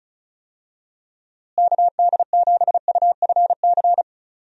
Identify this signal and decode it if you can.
Morse code, heard as a keyed tone, about 36 wpm, 700 Hz. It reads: KD7UFC
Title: USA Call Signs - 6 Characters - Repeated 3x - 35wpm